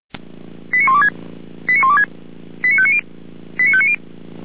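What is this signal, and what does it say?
Five-tone selective call (ZVEI), heard as quick runs of five tones, tones of about 70 ms. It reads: 89126 89126 89590 89590
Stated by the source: ZVEI five-tone selective call, code 89340, received by SDR.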